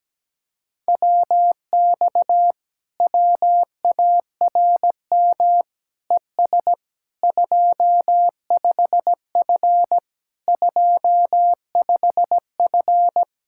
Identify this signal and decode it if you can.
Morse code, heard as a keyed tone, about 17 wpm, 700 Hz. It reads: WX WARM ES 25F 25F
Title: Speed-Racing Common QSO Phrases - 17wpm